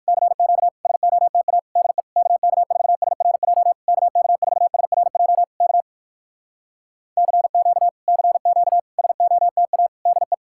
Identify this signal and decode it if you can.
Morse code, heard as a keyed tone, about 67 wpm, spaced as for 59 wpm, 700 Hz. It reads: CQ SOTA DE KK4SRJ KK4SRJ K CQ CQ SOTA DE